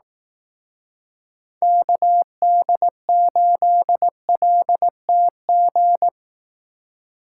Morse code, heard as a keyed tone, about 18 wpm, 700 Hz. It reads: KD8LTG
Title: Speed Racing - USA 2x3 Call Signs 15wpm